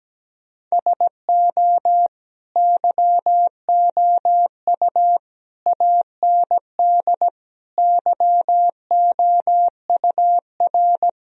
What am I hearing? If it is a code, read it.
Morse code, heard as a keyed tone, about 17 wpm, 700 Hz. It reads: SO YOU AND YOUR
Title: Sentences from Top 1000 Words - Review Part 4 - 17wpm